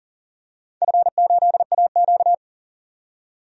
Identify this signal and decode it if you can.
Morse code, heard as a keyed tone, about 40 wpm, 700 Hz. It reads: F8AQ
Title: DXCC-100 Call Signs - 4 Characters - 40wpm